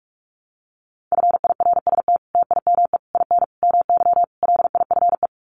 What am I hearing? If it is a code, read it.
Morse code, heard as a keyed tone, about 44 wpm, 700 Hz. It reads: FIRST TIME IN MY LIFE
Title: Sets of 5 Words Encore2 45wpm